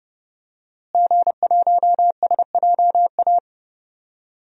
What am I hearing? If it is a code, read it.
Morse code, heard as a keyed tone, about 30 wpm, 700 Hz. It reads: G1SJA
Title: DXCC-100 Call Signs - 5 Characters - Repeated 2x - 30wpm